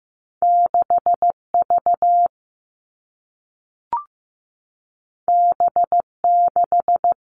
Morse code, heard as a keyed tone, about 15 wpm, 700 Hz. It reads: 6 V B 6
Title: Two Challenging Letter-Number Combinations 15wpm